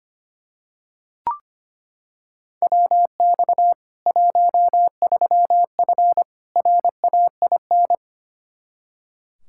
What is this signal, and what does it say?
Morse code, heard as a keyed tone, about 25 wpm, 700 Hz. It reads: WX 13F RAIN
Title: Practice QSO - Exchange 2 - Info - 25wpm